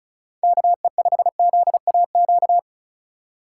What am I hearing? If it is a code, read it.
Morse code, heard as a keyed tone, about 36 wpm, 700 Hz. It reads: KE5ZAQ
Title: USA Call Signs - 6 Characters - 4x Character Spacing - 35wpm